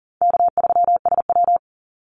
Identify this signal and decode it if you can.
Morse code, heard as a keyed tone, about 40 wpm, 700 Hz. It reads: K3SW
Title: Easy Call Signs Repeated 2x - 40wpm